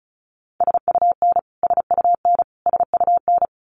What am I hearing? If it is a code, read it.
Morse code, heard as a keyed tone, about 34 wpm, 700 Hz. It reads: SUN SUN SUN
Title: Top 300 Words Repeated 35wpm